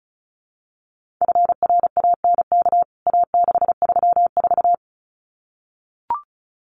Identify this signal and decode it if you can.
Morse code, heard as a keyed tone, about 34 wpm, 700 Hz. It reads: FRANK A634
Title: CWT Exchange 35wpm